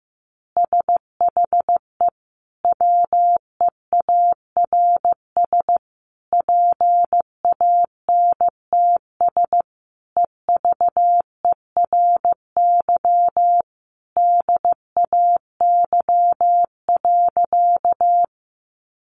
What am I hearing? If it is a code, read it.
Morse code, heard as a keyed tone, about 15 wpm, 700 Hz. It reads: SHE WEARS PANTS EVERY DAY.
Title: Additional Easy Phrases 15wpm